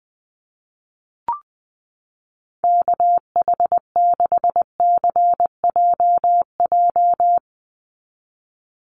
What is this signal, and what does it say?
Morse code, heard as a keyed tone, about 20 wpm, 700 Hz. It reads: KH6CJJ
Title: Call Signs Repeated 2x - 20wpm